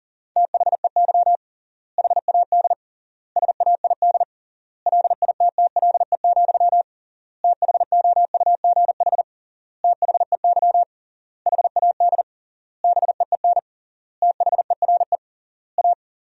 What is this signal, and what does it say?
Morse code, heard as a keyed tone, about 37 wpm, 700 Hz. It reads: THEY HAD SAID LITTLE, THOUGH THEY HAD BEEN THERE A
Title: 3x Word-Spacing - Sentences from Top 400 Words - 40wpm